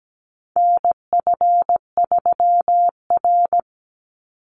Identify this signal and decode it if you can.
Morse code, heard as a keyed tone, about 17 wpm, 700 Hz. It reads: NF3R
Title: Call Signs Repeated 2x - 17wpm